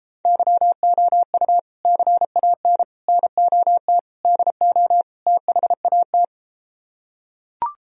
Morse code, heard as a keyed tone, about 33 wpm, 700 Hz. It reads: YOU CAN NOT DO THAT